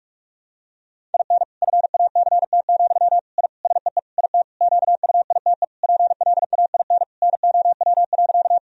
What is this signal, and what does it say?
Morse code, heard as a keyed tone, about 46 wpm, 700 Hz. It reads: IN FACT, I SEE IT QUITE PLAIN NOW.
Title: Sentences from Top 1000 Words - Review Part 3 - 45wpm